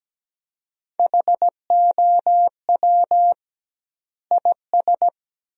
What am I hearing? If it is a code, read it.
Morse code, heard as a keyed tone, about 17 wpm, 700 Hz. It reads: HOW IS
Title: 2x Word-Spacing - Sentences from Top 100 Words - 17wpm